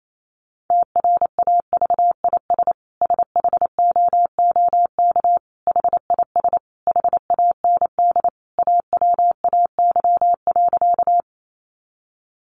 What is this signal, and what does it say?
Morse code, heard as a keyed tone, about 28 wpm, 700 Hz. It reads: TRAVIS SHOOK HIS HAND AWAY.
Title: A Sound Of Thunder 28wpm